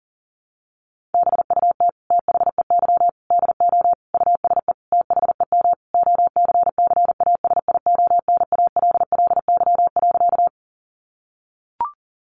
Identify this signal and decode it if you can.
Morse code, heard as a keyed tone, about 40 wpm, 700 Hz. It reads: BUT THEY DO USE THEM OCCASIONALLY.
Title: Sentences from Top 1300 Words 40wpm